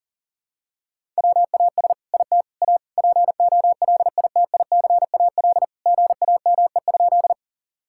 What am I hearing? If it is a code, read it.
Morse code, heard as a keyed tone, about 37 wpm, 700 Hz. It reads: WAS IT A POLITICAL GAME?
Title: Sentences from Top 1000 Words - Review Part 5 - 40wpm